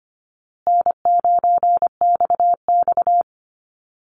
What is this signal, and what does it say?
Morse code, heard as a keyed tone, about 25 wpm, 700 Hz. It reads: N9XX
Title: Easy Call Signs Repeated 3x - 25wpm